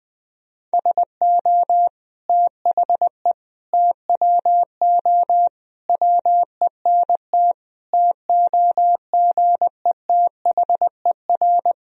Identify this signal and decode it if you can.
Morse code, heard as a keyed tone, about 21 wpm, 700 Hz. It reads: SO THE TWO WENT TOGETHER
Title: Sentences from Top 1000 Words - Review Part 1 - 20wpm